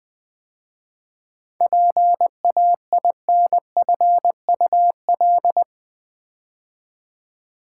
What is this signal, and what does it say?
Morse code, heard as a keyed tone, about 20 wpm, 690 Hz. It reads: PAINFUL
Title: 7 Letter Words - First 3 Letters Given - 20wpm